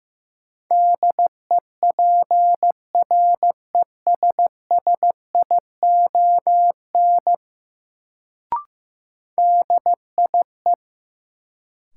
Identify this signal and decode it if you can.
Morse code, heard as a keyed tone, about 15 wpm, 700 Hz. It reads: DEPRESSION DIE
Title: Sentences from Top 1300 Words - New Word Drill Alt - 15wpm